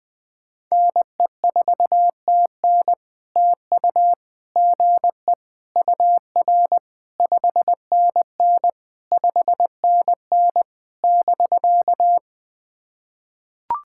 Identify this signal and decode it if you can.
Morse code, heard as a keyed tone, about 19 wpm, 700 Hz. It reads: NE4TN TU GE UR 5NN 5NN <BK>
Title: POTA QSO Practice - 20wpm